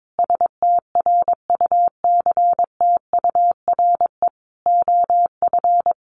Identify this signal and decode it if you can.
Morse code, heard as a keyed tone, about 22 wpm, 700 Hz. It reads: STRUCTURE OF
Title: Sentences from Top 1500 Words 22wpm